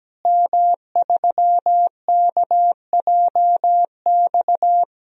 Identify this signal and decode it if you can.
Morse code, heard as a keyed tone, about 17 wpm, 700 Hz. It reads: M3KJX
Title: DXCC-100 Call Signs - 5 Characters - Repeated 3x - 17wpm